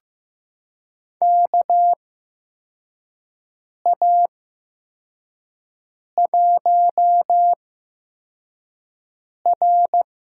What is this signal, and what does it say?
Morse code, heard as a keyed tone, about 15 wpm, spaced as for 4 wpm, 700 Hz. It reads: KA1R
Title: USA Call Signs - 6 Characters - 8x Character Spacing - 15wpm